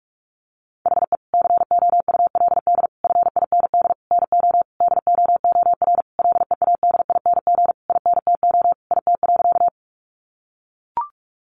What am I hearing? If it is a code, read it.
Morse code, heard as a keyed tone, about 44 wpm, 700 Hz. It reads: HE COULD FIND NO DOOR LEADING INTO IT.
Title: Sentences from Top 1100 Words 45wpm